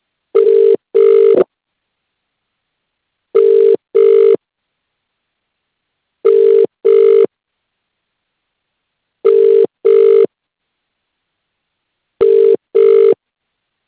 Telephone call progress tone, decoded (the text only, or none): ringback tone